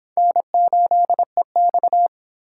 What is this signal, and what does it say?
Morse code, heard as a keyed tone, about 26 wpm, 700 Hz. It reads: N8EX